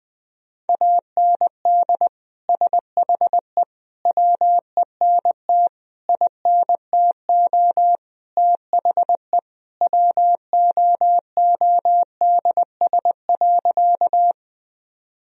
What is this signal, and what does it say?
Morse code, heard as a keyed tone, about 20 wpm, 700 Hz. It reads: AND SHE WENT INTO THE WOODS.
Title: Sentences from Top 1000 Words - Review Part 2 - 20wpm